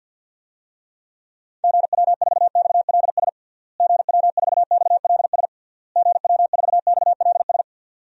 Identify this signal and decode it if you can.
Morse code, heard as a keyed tone, about 52 wpm, 700 Hz. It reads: GW4XLS GW4XLS GW4XLS
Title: DXCC-100 Call Signs - 6 Characters - Repeated 3x - 50wpm